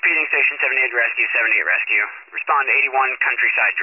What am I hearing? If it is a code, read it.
Emergency Alert System (SAME) tones, heard as harsh buzzing bursts, with speech. It is heard from 0.7 to 3.5 s.
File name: wcalert